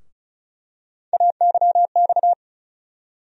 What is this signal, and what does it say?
Morse code, heard as a keyed tone, about 34 wpm, 700 Hz. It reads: AYX
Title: USA Call Sign Postfixes - 3 Letters 35wpm